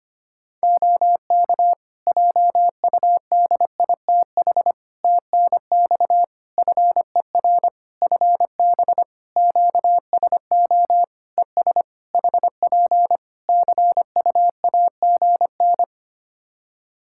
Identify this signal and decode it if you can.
Morse code, heard as a keyed tone, about 25 wpm, 700 Hz. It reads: OK JUDITH TNX FER FB QSO ES HP CUAGN
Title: Practice QSO - Exchange 4 - Ending - 25wpm